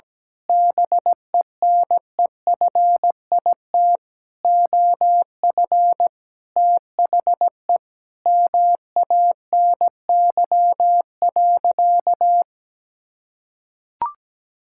Morse code, heard as a keyed tone, about 17 wpm, 700 Hz. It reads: BENEFIT OF THE MANY.
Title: Sentences from Top 1000 Words - Review Part 6 - 17wpm